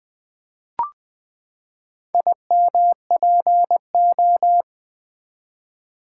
Morse code, heard as a keyed tone, about 20 wpm, 700 Hz. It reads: IMPO
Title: Build A Word - 4 Letters At A Time - 10-Letter Words - 20wpm